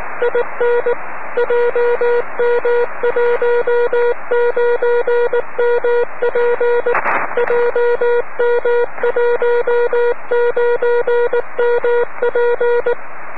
Morse code, heard as a keyed tone, about 19 wpm, 480 Hz. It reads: IN JM19MP JM19MP
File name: om3pv 70mhz / zaznam / ea6vq